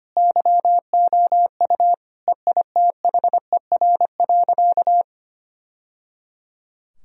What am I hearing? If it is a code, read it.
Morse code, heard as a keyed tone, about 25 wpm, 690 Hz. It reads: YOU EITHER.
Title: Sentences from Top 1000 Words - Review Part 4 - 25wpm